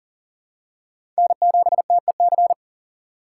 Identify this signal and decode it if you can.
Morse code, heard as a keyed tone, about 40 wpm, 700 Hz. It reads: N7TEC
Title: USA Call Signs - 5 Characters - 4x Character Spacing - 40wpm